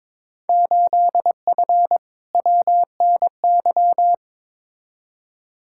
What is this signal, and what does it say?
Morse code, heard as a keyed tone, about 22 wpm, 700 Hz. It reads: 8F WNY
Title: Field Day Exchanges Encore - 22wpm